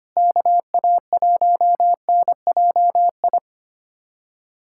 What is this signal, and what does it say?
Morse code, heard as a keyed tone, about 25 wpm, 700 Hz. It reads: KA1NJI
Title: USA Callsign DE Callsign - 6 Characters - 25wpm